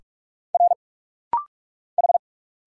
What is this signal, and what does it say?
Morse code, heard as a keyed tone, about 45 wpm, 700 Hz. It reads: R H
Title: Single Letter-Number - Rapid-Fire 45wpm